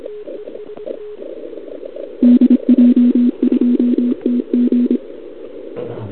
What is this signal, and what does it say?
Morse code, heard as a keyed tone, about 26 wpm, 240 Hz. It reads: DJ2TG
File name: CQWWCW2021 / Doderna OL3Z